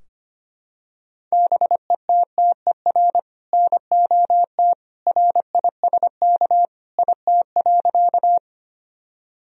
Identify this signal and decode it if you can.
Morse code, heard as a keyed tone, about 25 wpm, 700 Hz. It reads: BETTER NOT RISK IT.
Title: Sentences from Top 1000 Words - Review Part 4 - 25wpm